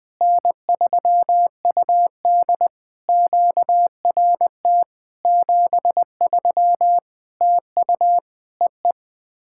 Morse code, heard as a keyed tone, about 20 wpm, 700 Hz. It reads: N3UD QRT 73 TU EE
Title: POTA QSO Practice - 20wpm